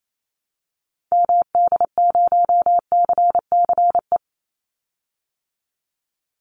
Morse code, heard as a keyed tone, about 28 wpm, 700 Hz. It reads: MD0CCE
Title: Call Signs Repeated 2x - 28wpm